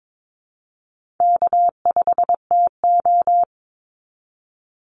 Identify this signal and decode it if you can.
Morse code, heard as a keyed tone, about 22 wpm, 700 Hz. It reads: K5TO
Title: Easy Call Signs Repeated 2x - 22wpm